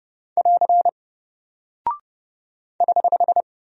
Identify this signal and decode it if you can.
Morse code, heard as a keyed tone, about 30 wpm, 700 Hz. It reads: <AR> <HH>
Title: Single Character (Letter:Number:Punctuation:Prosign) 30wpm